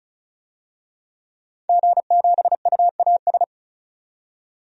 Morse code, heard as a keyed tone, about 36 wpm, 700 Hz. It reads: G7UAS
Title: DXCC-100 Call Signs - 5 Characters - 35wpm